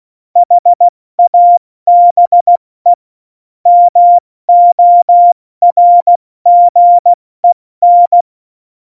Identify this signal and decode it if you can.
Morse code, heard as a keyed tone, about 16 wpm, 700 Hz. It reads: HABE MORGEN
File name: Morsekurs / Witze / MP3, Witz 24 April 2018, 70 F80